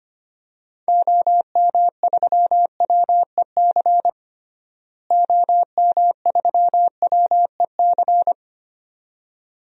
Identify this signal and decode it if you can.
Morse code, heard as a keyed tone, about 25 wpm, 700 Hz. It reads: OM3WEC OM3WEC
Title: DXCC-100 Call Signs - 6 Characters - Repeated 2x - 25wpm